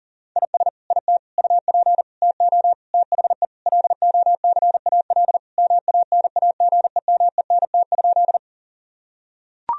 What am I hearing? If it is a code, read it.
Morse code, heard as a keyed tone, about 37 wpm, 700 Hz. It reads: IS IT UP TO THE LOCAL MANAGEMENT?
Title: Sentences from Top 1000 Words - Review Part 4 - 40wpm